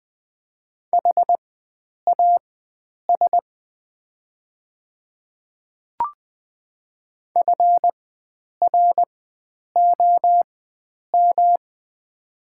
Morse code, heard as a keyed tone, about 20 wpm, spaced as for 9 wpm, 700 Hz. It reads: HAS FROM
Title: 4x Character Spacing - Top 100 Words - 20wpm